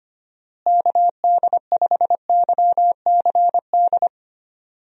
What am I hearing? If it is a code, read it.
Morse code, heard as a keyed tone, about 25 wpm, 700 Hz. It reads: KD5YCD
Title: USA Call Signs - 6 Characters - Repeated 2x - 25wpm